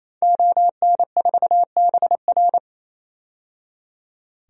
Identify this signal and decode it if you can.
Morse code, heard as a keyed tone, about 28 wpm, 700 Hz. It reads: ON4BR
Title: DXCC-100 Call Signs - 5 Characters - 28wpm